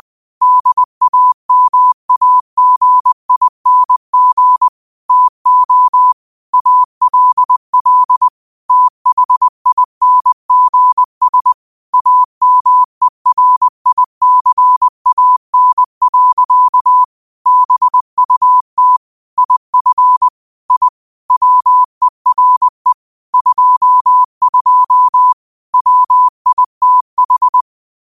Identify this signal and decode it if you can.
Morse code, heard as a keyed tone, about 20 wpm, 1000 Hz. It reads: DAMAGING TO ALL THINGS AMERICAN. BUT IF I WERE 22 WITH
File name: qotdfiles / 20wpm